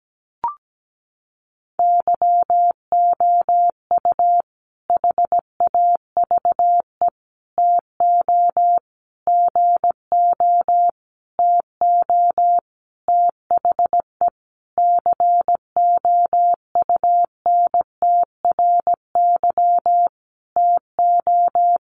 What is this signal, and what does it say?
Morse code, heard as a keyed tone, about 17 wpm, 700 Hz. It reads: YOU HAVE TO GO TO THE COUNTRY TO
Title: Sentences from Top 1300 Words 17wpm